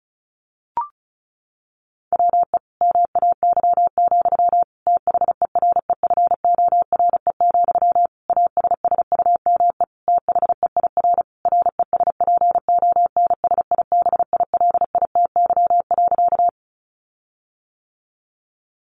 Morse code, heard as a keyed tone, about 34 wpm, 700 Hz. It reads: WE MAY, THEREFORE, ASSUME THEIR RESPONSIBILITY.
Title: Sentences from Top 1500 Words 35wpm